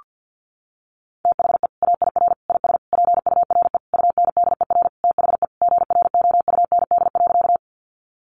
Morse code, heard as a keyed tone, about 50 wpm, 700 Hz. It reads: THE AIR IS PURE UNDER THE GROUND.
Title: Sentences from Top 1500 Words 50wpm